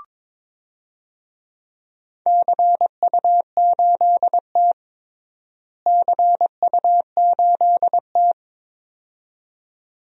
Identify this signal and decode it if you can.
Morse code, heard as a keyed tone, about 22 wpm, 700 Hz. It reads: CU8T CU8T
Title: DXCC-100 Call Signs - 4 Characters - Repeated 2x - 22wpm